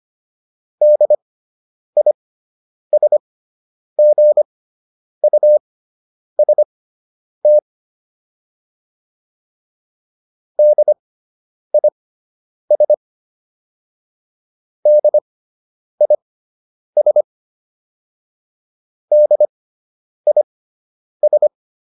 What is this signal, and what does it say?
Morse code, heard as a keyed tone, about 25 wpm, spaced as for 9 wpm, 600 Hz. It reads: DISGUST DIS DIS DIS